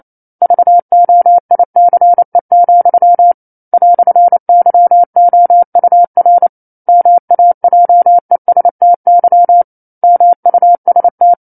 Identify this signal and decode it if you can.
Morse code, heard as a keyed tone, about 28 wpm, 700 Hz. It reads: VOICE, "YOUR MAJESTY MUST
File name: 09-Chapter-09 28 WPM